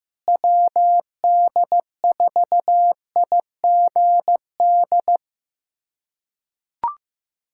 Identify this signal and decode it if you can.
Morse code, heard as a keyed tone, about 15 wpm, 700 Hz. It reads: WD4IGD
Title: USA Callsign DE Callsign 15wpm